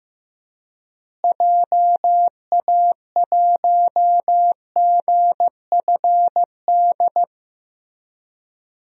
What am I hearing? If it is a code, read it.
Morse code, heard as a keyed tone, about 15 wpm, 700 Hz. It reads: JA1GFD
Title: DXCC-100 Call Signs - 6 Characters - Repeated 3x - 15wpm